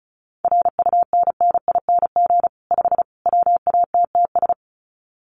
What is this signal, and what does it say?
Morse code, heard as a keyed tone, about 34 wpm, 700 Hz. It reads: RUNNING 5 WATTS
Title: Common QSO Phrases 35wpm